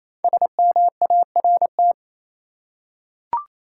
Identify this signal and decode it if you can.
Morse code, heard as a keyed tone, about 28 wpm, 700 Hz. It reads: SMART
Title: Five Letter Words - 28wpm